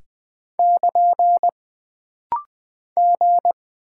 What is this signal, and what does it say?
Morse code, heard as a keyed tone, about 20 wpm, 700 Hz. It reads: <KN> G
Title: Single Character (Letter:Number:Punctuation:Prosign) - Rapid-Fire - 20wpm